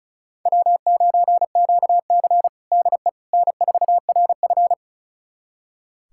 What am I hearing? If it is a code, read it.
Morse code, heard as a keyed tone, about 34 wpm, 700 Hz. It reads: W9QC DE N4RF
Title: USA Callsign DE Callsign - 4 Characters - 35wpm